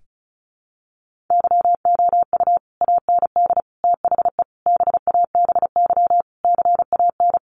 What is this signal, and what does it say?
Morse code, heard as a keyed tone, about 34 wpm, 700 Hz. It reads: YOU AND THE BABY CAN
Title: Sentences from Top 1300 Words 35wpm